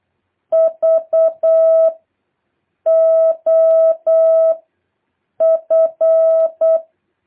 Morse code, heard as a keyed tone, about 8 wpm, 650 Hz. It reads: V O F
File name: jh8cbh / 2023cw